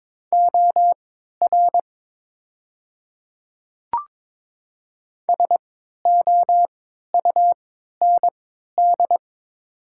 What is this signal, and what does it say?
Morse code, heard as a keyed tone, about 22 wpm, spaced as for 13 wpm, 700 Hz. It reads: OR SOUND
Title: 3x Character Spacing - Top 100 Words - 22wpm